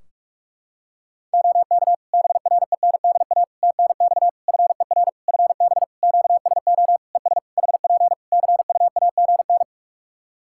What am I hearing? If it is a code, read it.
Morse code, heard as a keyed tone, about 41 wpm, 700 Hz. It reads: OK BRENDA TNX FER FB QSO ES HP CUAGN
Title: Practice QSO - Exchange 4 - Ending - 45wpm